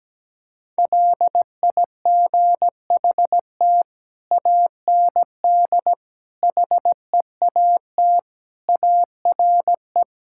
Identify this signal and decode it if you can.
Morse code, heard as a keyed tone, about 17 wpm, 700 Hz. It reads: LIGHT AND HEAT ARE
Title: Sentences from Top 1000 Words - Review Part 5 - 17wpm